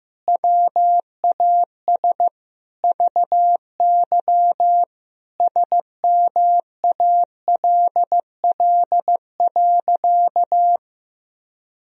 Morse code, heard as a keyed tone, about 15 wpm, 700 Hz. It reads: WAS VY SMALL.
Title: Sentences from Top 400 Words and Abbreviations 15wpm